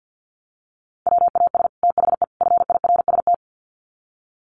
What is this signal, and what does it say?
Morse code, heard as a keyed tone, about 50 wpm, 700 Hz. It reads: WAS THE FIRST
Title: Sets of 3 Words 50wpm